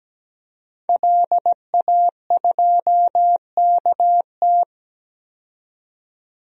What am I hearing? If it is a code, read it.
Morse code, heard as a keyed tone, about 17 wpm, 700 Hz. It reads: LA2KT
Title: DXCC-100 Call Signs - 5 Characters - 17wpm